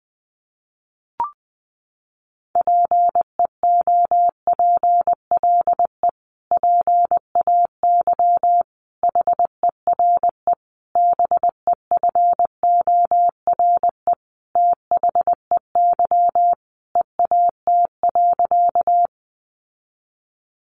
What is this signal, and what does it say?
Morse code, heard as a keyed tone, about 19 wpm, 700 Hz. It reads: PEOPLE PAY HERE BEFORE THEY EAT.
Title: Sentences from Top 700 Words 20wpm